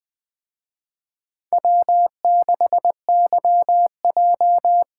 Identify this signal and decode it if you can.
Morse code, heard as a keyed tone, about 20 wpm, 700 Hz. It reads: W6YJ
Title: Speed Racing - USA 1x2 Call Signs 17wpm